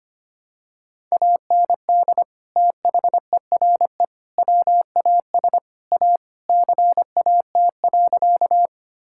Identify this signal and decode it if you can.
Morse code, heard as a keyed tone, about 25 wpm, 700 Hz. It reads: AND THERE WAS A CAT.
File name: practice / 25wpm / Sentences with 4-Word Phrases - 25wpm